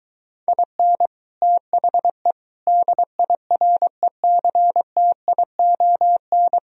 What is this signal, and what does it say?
Morse code, heard as a keyed tone, about 22 wpm, 700 Hz. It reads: IN THE DIRECTION